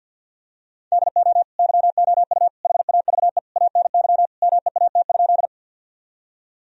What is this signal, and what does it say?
Morse code, heard as a keyed tone, about 52 wpm, 700 Hz. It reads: DO YOU HAVE ANY MEAT?
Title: Sentences from Top 1000 Words - Review Part 3 - 50wpm